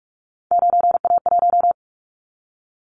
Morse code, heard as a keyed tone, about 44 wpm, 700 Hz. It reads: 9A1
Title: Callsign Prefixes for Top 25 DXCC - Repeated 3x - 45wpm